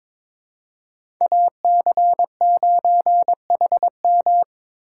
Morse code, heard as a keyed tone, about 21 wpm, 700 Hz. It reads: AC9HM
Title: USA Call Signs - 5 Characters - 6x Character Spacing - 22wpm